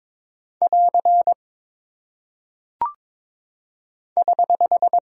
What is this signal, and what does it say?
Morse code, heard as a keyed tone, about 22 wpm, 700 Hz. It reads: <AR> <HH>
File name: practice / 22wpm / Single Punctuation-Prosign - 22wpm